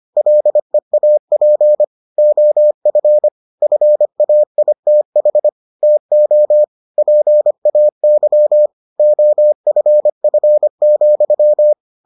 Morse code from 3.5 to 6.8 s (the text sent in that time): FAITH TO